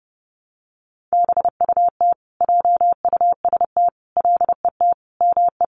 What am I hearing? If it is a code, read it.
Morse code, heard as a keyed tone, about 30 wpm, 700 Hz. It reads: BUT JUST LET ME
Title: Sentences from Top 1100 Words 30wpm